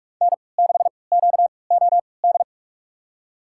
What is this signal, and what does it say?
Morse code, heard as a keyed tone, about 46 wpm, spaced as for 26 wpm, 700 Hz. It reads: N6QOD
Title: USA Call Signs - 3x Character Spacing - 45wpm